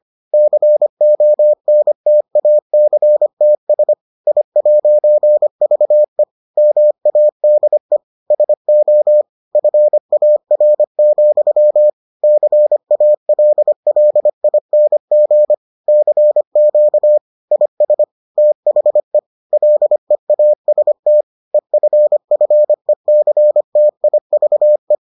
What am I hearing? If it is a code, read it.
Morse code, heard as a keyed tone, about 25 wpm, 600 Hz. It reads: CONTACTS I'VE MADE SO FAR, CALLING CQ IS THE LEAST EFFECTIVE